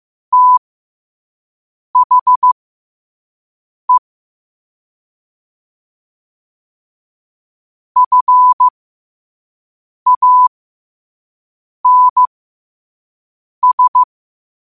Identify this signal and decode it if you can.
Morse code, heard as a keyed tone, about 15 wpm, spaced as for 5 wpm, 1000 Hz. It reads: THE FANS
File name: 5wpm